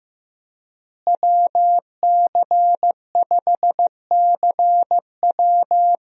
Morse code, heard as a keyed tone, about 15 wpm, 700 Hz. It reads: WC5CW
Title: USA Call Signs - 5 Characters - Repeated 3x - 15wpm